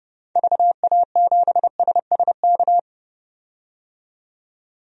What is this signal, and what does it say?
Morse code, heard as a keyed tone, about 30 wpm, 700 Hz. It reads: VA7SSK
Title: DXCC-100 Call Signs - 6 Characters - 30wpm